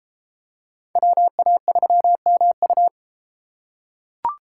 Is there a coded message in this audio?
Yes: Morse code from 1.0 to 2.9 s, heard as a keyed tone, about 33 wpm, 700 Hz. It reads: WA3MU